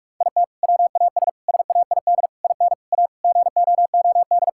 Morse code, heard as a keyed tone, about 44 wpm, 700 Hz. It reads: IT WAS SAID IN A GOOD
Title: Sentences from Top 1600 Words 45wpm